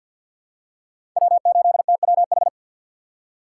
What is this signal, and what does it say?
Morse code, heard as a keyed tone, about 50 wpm, 700 Hz. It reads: W8TWH
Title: USA Call Signs - 6x Character Spacing - 50wpm